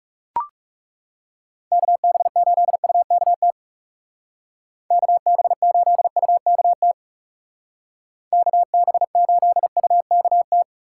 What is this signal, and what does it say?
Morse code, heard as a keyed tone, about 36 wpm, 700 Hz. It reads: KB8UKT KB8UKT KB8UKT
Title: Speed Racing - USA 2x3 Call Signs 30wpm